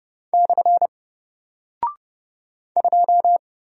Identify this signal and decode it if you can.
Morse code, heard as a keyed tone, about 30 wpm, 700 Hz. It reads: / 2
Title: Single Character (Letter:Number:Punctuation:Prosign) Alt 30wpm